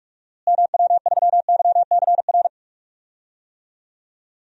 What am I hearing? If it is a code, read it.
Morse code, heard as a keyed tone, about 44 wpm, 700 Hz. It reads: MW3YCR
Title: DXCC-100 Call Signs - 6 Characters - Repeated 3x - 45wpm